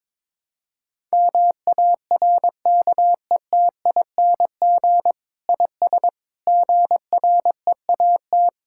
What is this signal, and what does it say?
Morse code, heard as a keyed tone, about 22 wpm, 700 Hz. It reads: MARKETING IS GREAT
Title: Sentences from Top 1000 Words - Review Part 5 - 22wpm